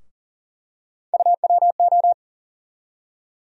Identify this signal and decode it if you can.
Morse code, heard as a keyed tone, about 40 wpm, 700 Hz. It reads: UWO
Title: USA Call Sign Postfixes - 3 Letters 40wpm